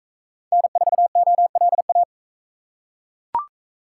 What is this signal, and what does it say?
Morse code, heard as a keyed tone, about 41 wpm, 700 Hz. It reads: N4OLA